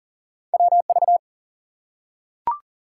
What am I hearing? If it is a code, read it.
Morse code, heard as a keyed tone, about 39 wpm, 700 Hz. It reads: WV
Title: US States and Canadian Provinces 40wpm